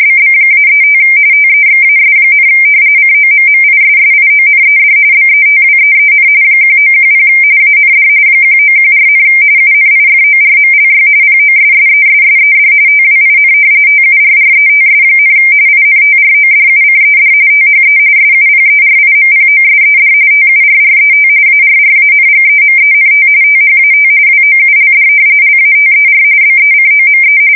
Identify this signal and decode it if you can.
RTTY, about 45.45 baud. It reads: BB-35 TEXAS, COMMISSIONED BETWEEN 1910 AND 1914, UNIFORMLY POSSESSED TWIN TURRETS, INTRODUCED THE SUPERIMPOSED TURRET ARRANGEMENT THAT WOULD LATER BECOME
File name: fsk170 / battleships